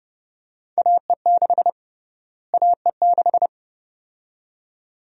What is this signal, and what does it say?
Morse code, heard as a keyed tone, about 31 wpm, 700 Hz. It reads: AE6 AE6
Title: USA Call Sign Prefixes - Repeated 2x - 30wpm